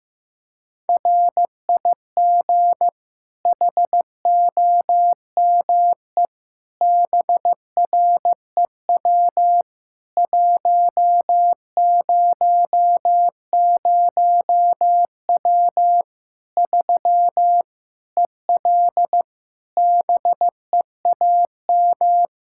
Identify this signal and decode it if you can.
Morse code, heard as a keyed tone, about 15 wpm, 700 Hz. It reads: RIG HOME BREW 100W 3 EL BEAM
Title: Practice QSO - Exchange 2 - Info - 15wpm